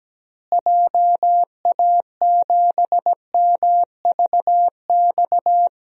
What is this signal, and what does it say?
Morse code, heard as a keyed tone, about 17 wpm, 700 Hz. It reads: JA7MVX
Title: DXCC-100 Call Signs - 6 Characters - Repeated 3x - 17wpm